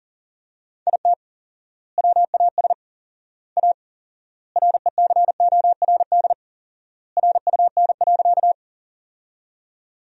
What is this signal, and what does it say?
Morse code, heard as a keyed tone, about 37 wpm, 700 Hz. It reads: IT WAS A RECORD RUN.
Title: 4x Word-Spacing - Sentences from Top 500 Words - 40wpm